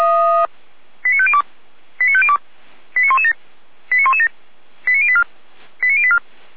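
Five-tone selective call (ZVEI), heard as quick runs of five tones, tones of about 70 ms. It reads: 89502 89502 89117 89117 89084 89084
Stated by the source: Five-tone ZVEI selective call 89092, received by SDR.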